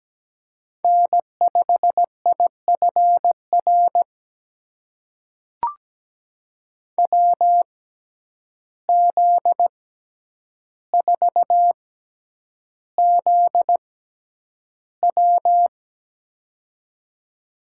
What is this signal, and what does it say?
Morse code, heard as a keyed tone, about 17 wpm, spaced as for 9 wpm, 700 Hz. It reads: N5IFR WZ4ZW
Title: USA Call Signs - 5 Characters - 6x Character Spacing - 17wpm